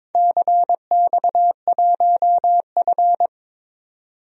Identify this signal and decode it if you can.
Morse code, heard as a keyed tone, about 22 wpm, 700 Hz. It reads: CX1F
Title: DXCC-100 Call Signs - 4 Characters - 22wpm